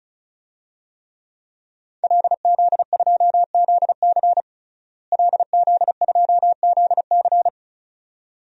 Morse code, heard as a keyed tone, about 36 wpm, 700 Hz. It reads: LZ2ZC LZ2ZC
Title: DXCC-100 Call Signs - 5 Characters - Repeated 2x - 35wpm